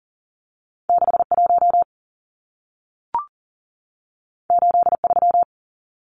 Morse code, heard as a keyed tone, about 40 wpm, 700 Hz. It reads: 61 83
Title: Numbers - 2-Digit 40wpm